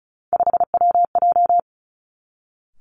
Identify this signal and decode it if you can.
Morse code, heard as a keyed tone, about 36 wpm, 700 Hz. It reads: 5WJ
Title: Three Challenging Letter-Number Combinations 35wpm